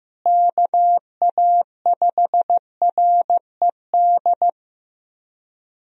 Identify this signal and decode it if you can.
Morse code, heard as a keyed tone, about 16 wpm, 700 Hz. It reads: KA5RED
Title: USA Call Signs - 6 Characters - Repeated 3x - 15wpm